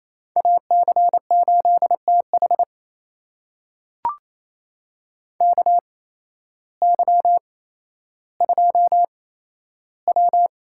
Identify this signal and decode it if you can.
Morse code, heard as a keyed tone, about 28 wpm, spaced as for 13 wpm, 700 Hz. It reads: AC8TH KY2W
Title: USA Call Signs - 8x Character Spacing - 28wpm